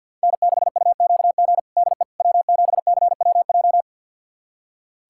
Morse code, heard as a keyed tone, about 53 wpm, 700 Hz. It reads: N6UQG DE W7CWJ
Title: USA Callsign DE Callsign - 5 Characters - 50wpm